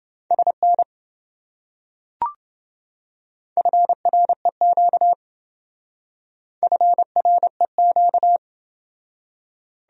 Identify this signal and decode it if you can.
Morse code, heard as a keyed tone, about 28 wpm, 700 Hz. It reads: SN FREQ FREQ